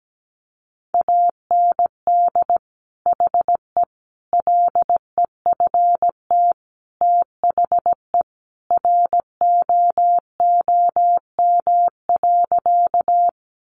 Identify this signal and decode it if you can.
Morse code, heard as a keyed tone, about 17 wpm, 700 Hz. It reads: AND HE LEFT THE ROOM.
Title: Sentences from Top 300 Words 17wpm